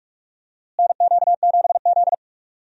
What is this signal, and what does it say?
Morse code, heard as a keyed tone, about 44 wpm, 700 Hz. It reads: NQ7Z
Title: USA Call Signs - 4 Characters - Repeated 3x - 45wpm